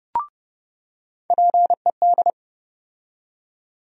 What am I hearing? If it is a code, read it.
Morse code, heard as a keyed tone, about 30 wpm, 700 Hz. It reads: PED
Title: USA Call Sign Postfixes - 3 Letters 30wpm